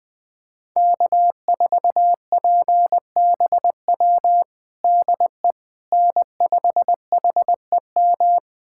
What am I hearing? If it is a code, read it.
Morse code, heard as a keyed tone, about 20 wpm, 700 Hz. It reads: K4PBW DE N5HEM
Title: USA Callsign DE Callsign - 5 Characters - 20wpm